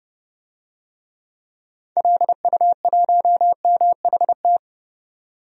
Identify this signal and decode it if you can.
Morse code, heard as a keyed tone, about 30 wpm, 700 Hz. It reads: LU1MHT
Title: DXCC-100 Call Signs - 6 Characters - 30wpm